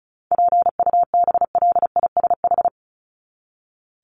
Morse code, heard as a keyed tone, about 34 wpm, 700 Hz. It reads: PUBLISH
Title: ED Suffix - Top 100 Words with Base Words 35wpm